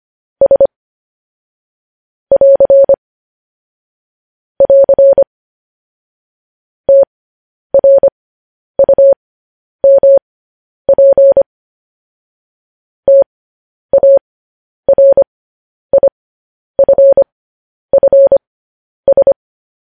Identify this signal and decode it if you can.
Morse code, heard as a keyed tone, about 25 wpm, spaced as for 10 wpm, 550 Hz. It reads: S <AR> <AR> TRUMP TARIFFS
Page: news headlines in morse code at 10 wpm